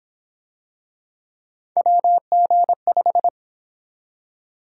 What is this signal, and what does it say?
Morse code, heard as a keyed tone, about 26 wpm, 700 Hz. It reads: WG5
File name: Speed-Racing - Three Letter-Number Combinations - 22wpm